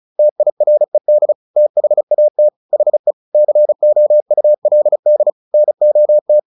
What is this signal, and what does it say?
Morse code, heard as a keyed tone, about 34 wpm, 600 Hz. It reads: TIRED THAT HE COULD NOT
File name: the old man and the sea / the old man and the sea 1 35wpm